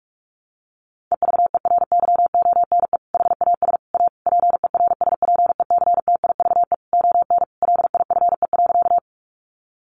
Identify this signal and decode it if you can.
Morse code, heard as a keyed tone, about 44 wpm, 700 Hz. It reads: EVERYONE HAS A PERSPECTIVE ON LIFE.
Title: Sentences from Top 1200 Words 45wpm